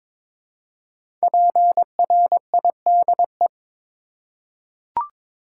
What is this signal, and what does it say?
Morse code, heard as a keyed tone, about 22 wpm, 700 Hz. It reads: PRIDE
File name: 5 Letter Words - First 3 Letters Given - 22wpm